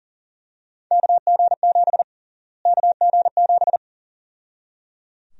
Morse code, heard as a keyed tone, about 37 wpm, 700 Hz. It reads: KG7 KG7
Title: USA Call Sign Prefixes - Repeated 2x - 40wpm